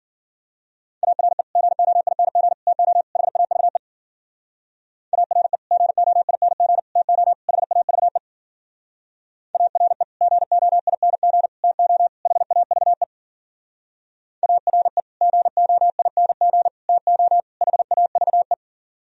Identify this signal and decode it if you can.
Morse code, heard as a keyed tone, about 50 wpm, 700 Hz. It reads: ARE GOING TO HAVE ARE GOING TO HAVE ARE GOING TO HAVE ARE GOING TO HAVE